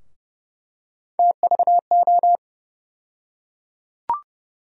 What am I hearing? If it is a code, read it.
Morse code, heard as a keyed tone, about 30 wpm, 700 Hz. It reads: TVO
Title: USA Call Sign Postfixes - 3 Letters 30wpm